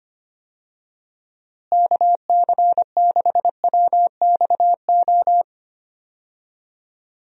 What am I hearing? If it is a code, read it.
Morse code, heard as a keyed tone, about 25 wpm, 700 Hz. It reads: KC6WXO
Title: USA Call Signs - 6 Characters - Repeated 2x - 25wpm